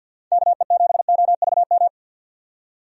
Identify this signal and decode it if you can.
Morse code, heard as a keyed tone, about 52 wpm, 700 Hz. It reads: KE7OVM
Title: USA Call Signs - 6 Characters - 50wpm